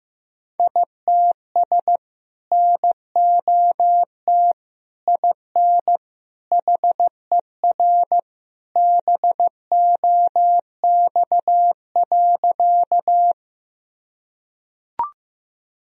Morse code, heard as a keyed tone, about 15 wpm, 700 Hz. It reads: ITS NOT IN HER BOX.
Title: Sentences from Top 1000 Words - Review Part 2 - 15wpm